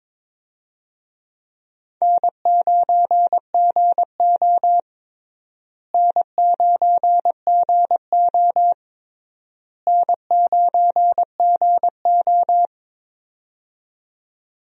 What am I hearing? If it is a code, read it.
Morse code, heard as a keyed tone, about 22 wpm, 700 Hz. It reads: N9GO N9GO N9GO
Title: USA Call Signs - 4 Characters - Repeated 3x - 22wpm